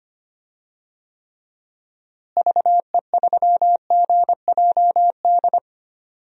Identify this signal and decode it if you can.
Morse code, heard as a keyed tone, about 25 wpm, 700 Hz. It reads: VE3GJD
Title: DXCC-100 Call Signs - 6 Characters - Repeated 2x - 25wpm